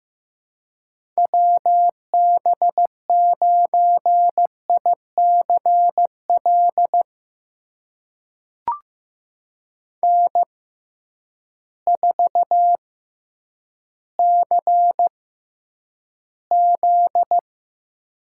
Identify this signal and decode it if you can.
Morse code, heard as a keyed tone, about 15 wpm, spaced as for 9 wpm, 700 Hz. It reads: WB9ICL N4CZ
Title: USA Call Signs - 6x Character Spacing - 15wpm